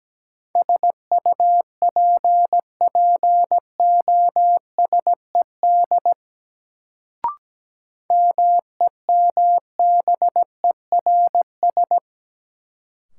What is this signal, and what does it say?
Morse code, heard as a keyed tone, about 17 wpm, 700 Hz. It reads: SUPPOSED MEMBERS
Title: Sets of 3 Words - New Word Drill - 17wpm